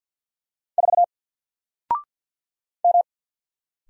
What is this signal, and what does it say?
Morse code, heard as a keyed tone, about 48 wpm, 700 Hz. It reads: 4 M
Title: Single Letter-Number Alt 50wpm